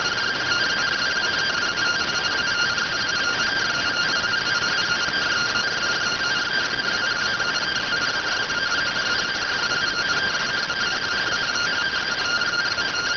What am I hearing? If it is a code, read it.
Two-tone teleprinter signal, about 170 Hz shift.